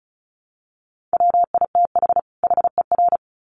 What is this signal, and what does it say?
Morse code, heard as a keyed tone, about 34 wpm, 700 Hz. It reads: WITH HER
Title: Sets of 2 Words 35wpm